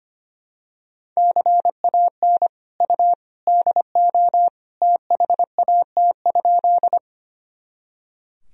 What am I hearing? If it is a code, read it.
Morse code, heard as a keyed tone, about 25 wpm, 700 Hz. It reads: CAN U DO THAT?
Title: Sentences from Top 100 Words and Abbreviations 25wpm